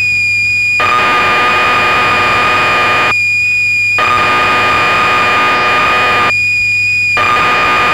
Cassette data (Kansas City type).
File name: magazines / Acorn User / 1984-12 BBC Programs